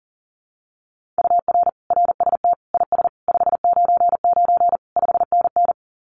Morse code, heard as a keyed tone, about 40 wpm, 700 Hz. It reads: UR RST IS 599 5NN
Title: Common QSO Phrases Encore 40wpm